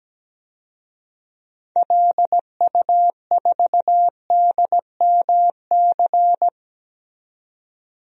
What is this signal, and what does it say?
Morse code, heard as a keyed tone, about 17 wpm, 700 Hz. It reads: LU4DMC
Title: DXCC-100 Call Signs - 6 Characters - 17wpm